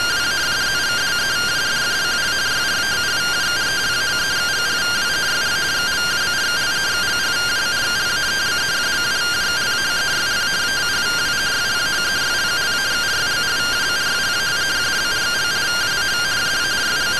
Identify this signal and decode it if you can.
Two-tone teleprinter signal, about 170 Hz shift.